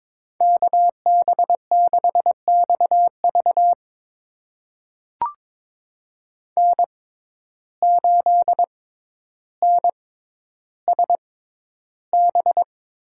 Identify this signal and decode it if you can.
Morse code, heard as a keyed tone, about 22 wpm, spaced as for 11 wpm, 700 Hz. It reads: KB6XV N8NSB
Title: USA Call Signs - 5 Characters - 6x Character Spacing - 22wpm